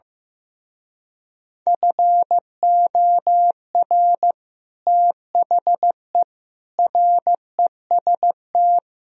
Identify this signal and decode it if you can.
Morse code, heard as a keyed tone, about 16 wpm, 700 Hz. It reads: FOR THE REST
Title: Speed-Racing - Sets of 5 Words 15wpm